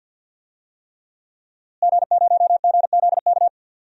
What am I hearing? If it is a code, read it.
Morse code, heard as a keyed tone, about 52 wpm, 700 Hz. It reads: G0GZK
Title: DXCC-100 Call Signs - 5 Characters - Repeated 3x - 50wpm